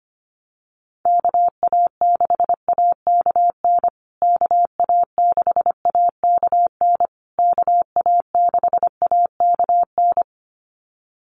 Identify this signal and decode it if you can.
Morse code, heard as a keyed tone, about 25 wpm, 700 Hz. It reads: KA6AKN KA6AKN KA6AKN
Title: USA Call Signs Repeated 3x 25wpm